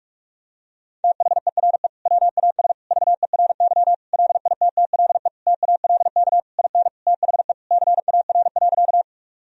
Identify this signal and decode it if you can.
Morse code, heard as a keyed tone, about 46 wpm, 700 Hz. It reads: THERE WAS VERY LITTLE TALK IN THE CAR.
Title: Sentences from Top 1000 Words - Review Part 1 - 45wpm